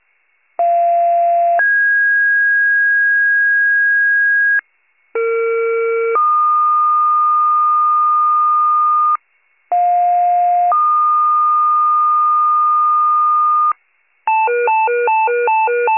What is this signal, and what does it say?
Two-tone page, heard as a one-second tone, then a three-second tone, about 690 then 1650 Hz, about 460 then 1220 Hz, about 710 then 1220 Hz. It is heard from 0.6 to 13.7 s.